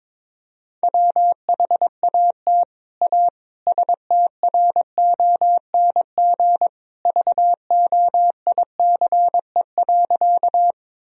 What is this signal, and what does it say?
Morse code, heard as a keyed tone, about 22 wpm, 700 Hz. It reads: WHAT A STRONG VOICE.
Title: Sentences from Top 1000 Words - Review Part 3 - 22wpm